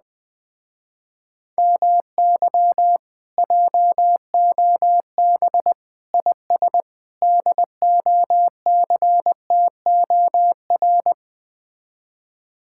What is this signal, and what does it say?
Morse code, heard as a keyed tone, about 20 wpm, 700 Hz. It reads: MY JOB IS DOCTOR
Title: Speed-Racing Common QSO Phrases - 17wpm